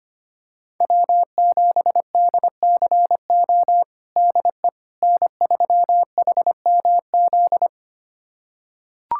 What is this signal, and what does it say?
Morse code, heard as a keyed tone, about 25 wpm, 700 Hz. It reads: W7DCO DE N3HMZ
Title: USA Callsign DE Callsign - 5 Characters - 25wpm